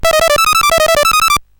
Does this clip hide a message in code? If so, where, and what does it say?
No.